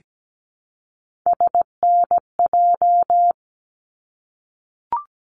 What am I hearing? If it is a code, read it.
Morse code, heard as a keyed tone, about 17 wpm, 700 Hz. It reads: SNJ
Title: ARRL Sections 17wpm